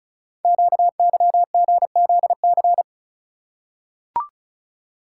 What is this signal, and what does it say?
Morse code, heard as a keyed tone, about 36 wpm, 700 Hz. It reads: QYGZC
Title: Code Groups - 5 Letters - 35wpm